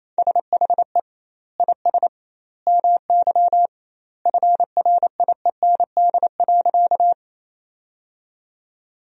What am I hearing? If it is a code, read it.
Morse code, heard as a keyed tone, about 28 wpm, 700 Hz. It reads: SHE IS MY FRIEND.
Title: 2x Word-Spacing - Sentences from Top 300 Words - 28wpm